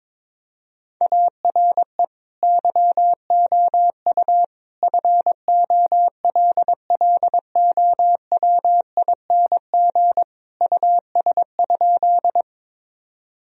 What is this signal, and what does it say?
Morse code, heard as a keyed tone, about 22 wpm, 700 Hz. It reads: ARE YOU FOLLOWING US?
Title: Sentences from Top 1000 Words - Review Part 4 - 22wpm